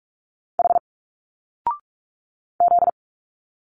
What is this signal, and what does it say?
Morse code, heard as a keyed tone, about 46 wpm, 700 Hz. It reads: H Z
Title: Single Letters 45wpm